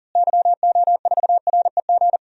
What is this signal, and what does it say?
Morse code, heard as a keyed tone, about 40 wpm, 700 Hz. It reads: YO4REG
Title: DXCC-100 Call Signs - 6 Characters - Repeated 3x - 40wpm